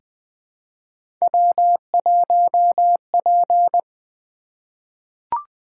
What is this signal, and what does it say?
Morse code, heard as a keyed tone, about 20 wpm, 700 Hz. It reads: W1P
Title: Three Letter-Number Combinations - 20wpm